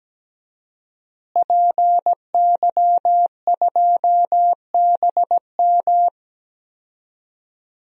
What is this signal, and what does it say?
Morse code, heard as a keyed tone, about 17 wpm, 700 Hz. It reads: PY2BM
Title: DXCC-100 Call Signs - 5 Characters - Repeated 3x - 17wpm